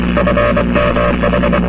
Morse code, heard as a keyed tone, about 24 wpm, 590 Hz, with noise. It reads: FMH